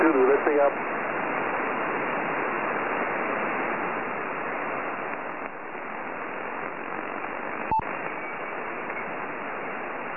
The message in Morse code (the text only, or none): none